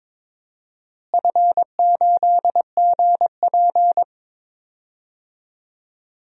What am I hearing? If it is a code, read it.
Morse code, heard as a keyed tone, about 22 wpm, 700 Hz. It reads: F8GP
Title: DXCC-100 Call Signs - 4 Characters - Repeated 2x - 22wpm